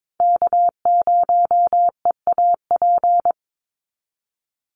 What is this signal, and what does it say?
Morse code, heard as a keyed tone, about 22 wpm, 700 Hz. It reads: K0EAP
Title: NA SOTA Chaser Callsigns 22wpm